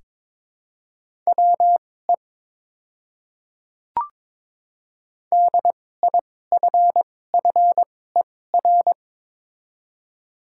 Morse code, heard as a keyed tone, about 22 wpm, spaced as for 16 wpm, 700 Hz. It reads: WE DIFFER
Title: 2x Character Spacing - Top 300 Words - 22wpm